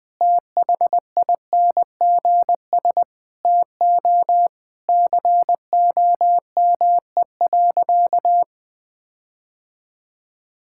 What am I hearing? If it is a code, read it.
Morse code, heard as a keyed tone, about 21 wpm, 700 Hz. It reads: THINGS TO COME.
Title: Sentences with 6-Word Phrases - 20wpm